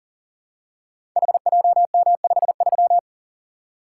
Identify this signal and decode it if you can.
Morse code, heard as a keyed tone, about 39 wpm, 690 Hz. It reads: HJM53
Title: Code Groups - 5 Letter:Numbers - 40wpm